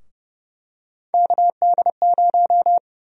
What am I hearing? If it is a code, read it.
Morse code, heard as a keyed tone, about 30 wpm, 700 Hz. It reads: KD0
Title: USA Call Sign Prefixes - Repeated 2x - 30wpm